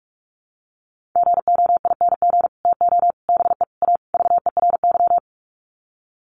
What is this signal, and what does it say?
Morse code, heard as a keyed tone, about 44 wpm, 700 Hz. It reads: GOING TO BE A VERY
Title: Sets of 5 Words Encore 45wpm